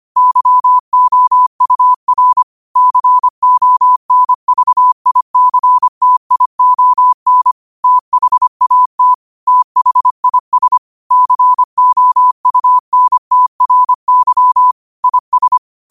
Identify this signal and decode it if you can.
Morse code, heard as a keyed tone, about 25 wpm, 1000 Hz. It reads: YOUR CONVICTION THAT THIS COUNTRY IS
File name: qotdfiles / 25wpm